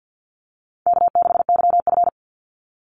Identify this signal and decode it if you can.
Morse code, heard as a keyed tone, about 50 wpm, 700 Hz. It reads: K6YF
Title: Easy Call Signs - 50wpm